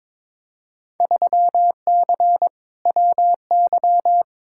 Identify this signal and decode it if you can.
Morse code, heard as a keyed tone, about 22 wpm, 700 Hz. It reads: 3C WY
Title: Field Day Exchanges Encore - 22wpm